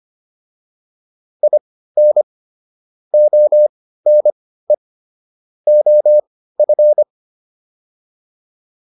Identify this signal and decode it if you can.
Morse code, heard as a keyed tone, about 25 wpm, spaced as for 15 wpm, 600 Hz. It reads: IN ONE OF